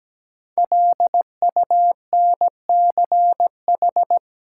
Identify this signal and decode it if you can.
Morse code, heard as a keyed tone, about 17 wpm, 700 Hz. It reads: LUNCH
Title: Five Letter Words - 17wpm